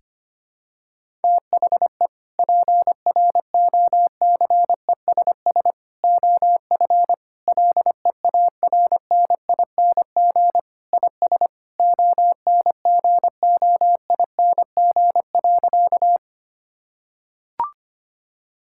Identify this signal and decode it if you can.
Morse code, heard as a keyed tone, about 25 wpm, 700 Hz. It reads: THE PROCESS OF LEARNING IS ONGOING.
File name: practice / 25wpm / Sentences with 3-Word Phrases - 25wpm